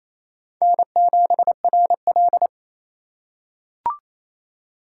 Morse code, heard as a keyed tone, about 28 wpm, 700 Hz. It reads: N7RL
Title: USA Call Signs - 4x Character Spacing - 28wpm